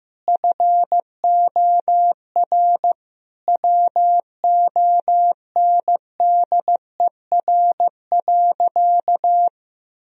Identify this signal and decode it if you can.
Morse code, heard as a keyed tone, about 15 wpm, 700 Hz. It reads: FOR WONDER.
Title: Sentences from Top 1000 Words - Review Part 3 - 15wpm